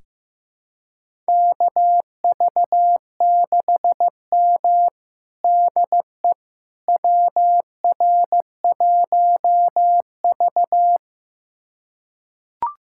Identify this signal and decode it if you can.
Morse code, heard as a keyed tone, about 15 wpm, 700 Hz. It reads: KV6M DE WR1V
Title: USA Callsign DE Callsign - 4 Characters - 15wpm